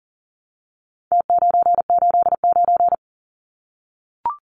Morse code, heard as a keyed tone, about 40 wpm, 700 Hz. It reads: T989
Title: Numbers - Contest Style - 4-Digit 40wpm